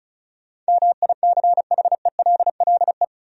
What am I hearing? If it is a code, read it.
Morse code, heard as a keyed tone, about 36 wpm, 700 Hz. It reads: MICHELLE
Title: USA Common Names - 35wpm